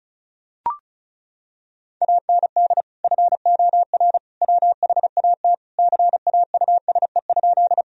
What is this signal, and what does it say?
Morse code, heard as a keyed tone, about 34 wpm, 700 Hz. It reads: AND FOR WHAT CAUSE?
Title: Sentences from Top 1000 Words - Review Part 1 - 35wpm